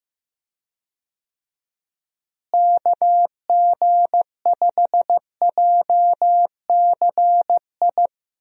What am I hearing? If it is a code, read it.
Morse code, heard as a keyed tone, about 15 wpm, 700 Hz. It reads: KG5JCI
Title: USA Call Signs - 6 Characters - Repeated 3x - 15wpm